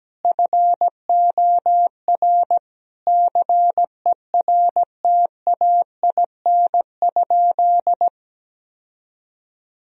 Morse code, heard as a keyed tone, about 17 wpm, 700 Hz. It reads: FOR CERTAIN?
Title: Sentences from Top 1000 Words - Review Part 2 - 17wpm